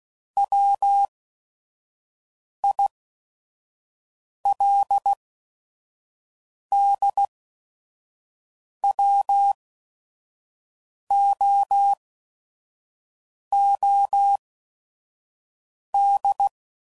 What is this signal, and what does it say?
Morse code, heard as a keyed tone, about 16 wpm, spaced as for 5 wpm, 800 Hz. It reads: WILDWOOD
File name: CW Code Groups @5WPM / fromSM's / Cities